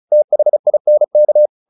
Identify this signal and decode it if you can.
Morse code, heard as a keyed tone, about 34 wpm, 600 Hz. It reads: THINK
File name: the old man and the sea / the old man and the sea 1 35wpm